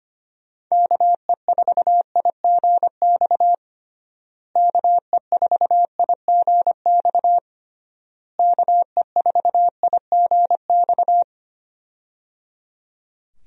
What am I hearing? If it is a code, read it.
Morse code, heard as a keyed tone, about 25 wpm, 700 Hz. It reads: KE4IGX KE4IGX KE4IGX
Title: USA Call Signs - 6 Characters - Repeated 3x - 25wpm